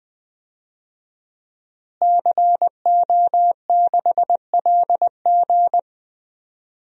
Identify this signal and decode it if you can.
Morse code, heard as a keyed tone, about 20 wpm, 700 Hz. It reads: CO6LG
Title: DXCC-100 Call Signs - 5 Characters - Repeated 3x - 20wpm